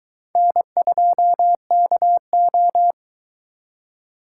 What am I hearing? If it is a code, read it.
Morse code, heard as a keyed tone, about 23 wpm, 700 Hz. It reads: N2KO